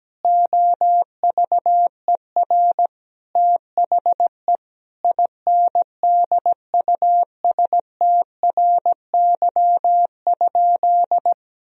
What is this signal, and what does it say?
Morse code, heard as a keyed tone, about 17 wpm, 700 Hz. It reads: OVER THE INDUSTRY?
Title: Sentences from Top 1000 Words - Review Part 4 - 17wpm